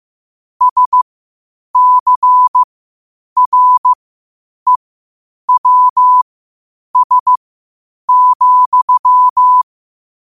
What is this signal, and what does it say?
Morse code, heard as a keyed tone, about 15 wpm, spaced as for 8 wpm, 1000 Hz. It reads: SCREWS,
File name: qotdfiles / 8wpm